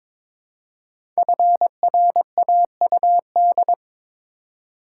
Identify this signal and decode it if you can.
Morse code, heard as a keyed tone, about 23 wpm, 700 Hz. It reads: FRAUD
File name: 5 Letter Words - First 3 Letters Given - 22wpm